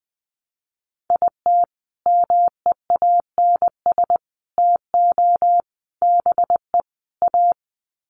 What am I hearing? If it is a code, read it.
Morse code, heard as a keyed tone, about 20 wpm, 700 Hz. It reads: IT MEANS TO BE A
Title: Sets of 5 Words Encore2 20wpm